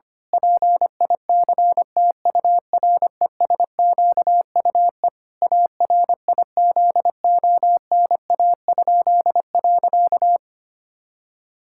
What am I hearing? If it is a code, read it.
Morse code, heard as a keyed tone, about 25 wpm, 700 Hz. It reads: PICTURESQUE ARIZONA?.
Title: Pangram Challenge v2 - 25wpm